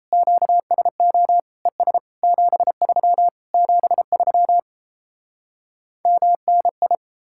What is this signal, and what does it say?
Morse code, heard as a keyed tone, about 38 wpm, 700 Hz. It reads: QSO ES 73 73 MNI